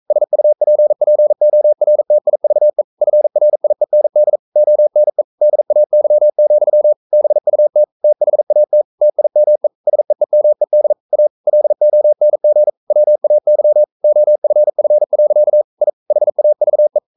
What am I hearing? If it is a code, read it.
Morse code, heard as a keyed tone, about 42 wpm, 600 Hz. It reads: SUPPORTIVE FRIEND ONE DAY, BUT THAT TIME SEEMED A LONG WAY OFF. I HAVE
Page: Romanen Never af Ken Follet på engelsk.